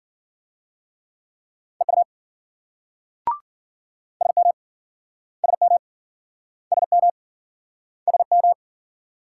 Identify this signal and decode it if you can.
Morse code, heard as a keyed tone, about 53 wpm, 700 Hz. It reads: EU SM SM SM SM